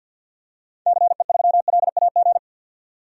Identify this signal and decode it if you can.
Morse code, heard as a keyed tone, about 52 wpm, 700 Hz. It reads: CE3LAG
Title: DXCC-100 Call Signs - 6 Characters - 50wpm